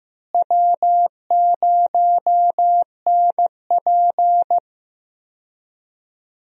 Morse code, heard as a keyed tone, about 16 wpm, 700 Hz. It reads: W0NP
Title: USA Call Signs - 4 Characters - Repeated 2x - 15wpm